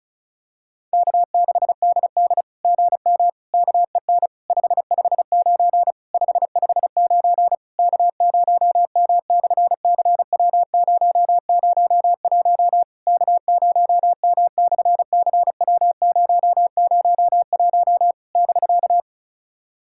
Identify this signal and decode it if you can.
Morse code, heard as a keyed tone, about 33 wpm, 700 Hz. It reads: K6DD GM KEN 559 559 K0M/CW001 K0M/CW001 <BK>
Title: SOTA QSO Practice 35wpm